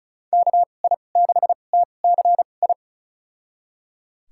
Morse code, heard as a keyed tone, about 36 wpm, spaced as for 25 wpm, 700 Hz. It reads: KI6TCI
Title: USA Call Signs - 6 Characters - 2x Character Spacing - 35wpm